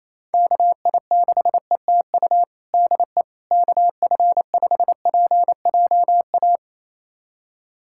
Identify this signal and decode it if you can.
Morse code, heard as a keyed tone, about 28 wpm, 700 Hz. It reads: KI6ETU DE KF5PJA
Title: USA Callsign DE Callsign - 6 Characters - 28wpm